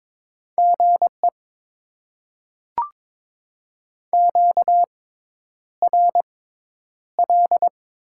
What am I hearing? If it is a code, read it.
Morse code, heard as a keyed tone, about 22 wpm, spaced as for 10 wpm, 700 Hz. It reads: GE QRL
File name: Top 20 QSO Elements - 6x Character Spacing - 22wpm